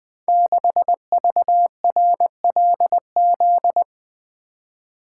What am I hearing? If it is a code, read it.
Morse code, heard as a keyed tone, about 20 wpm, 700 Hz. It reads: 6VRLZ
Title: Code Groups - 5 Letter:Numbers - 20wpm